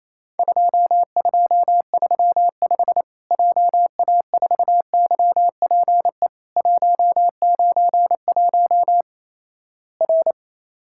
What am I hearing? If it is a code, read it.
Morse code, heard as a keyed tone, about 28 wpm, 700 Hz. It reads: 2235 JA4YPE 191 R
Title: Worked All Europe DX Contest - QTC Exchange - 28wpm